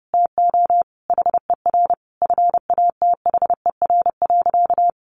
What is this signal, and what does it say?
Morse code, heard as a keyed tone, about 30 wpm, 700 Hz. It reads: TO HER FATHER.
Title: Sentences from Top 1500 Words 30wpm